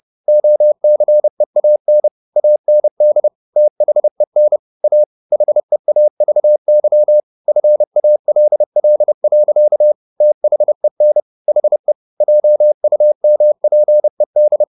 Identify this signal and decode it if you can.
Morse code, heard as a keyed tone, about 30 wpm, 600 Hz. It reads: OCEAN AND THEN A HEAVY FALL. THEN HE JUMPED